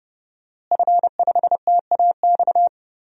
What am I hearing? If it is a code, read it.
Morse code, heard as a keyed tone, about 30 wpm, 700 Hz. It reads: F5TAX
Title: DXCC-100 Call Signs - 5 Characters - Repeated 2x - 30wpm